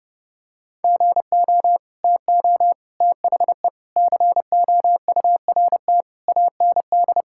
Morse code, heard as a keyed tone, about 32 wpm, 700 Hz. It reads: GO TO THE COURT AND
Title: Sentences from Top 1600 Words 30wpm